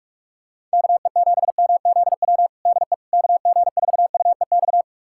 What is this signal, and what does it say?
Morse code, heard as a keyed tone, about 42 wpm, 700 Hz. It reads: KE7MZW DE KG4UEX
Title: USA Callsign DE Callsign - 6 Characters - 45wpm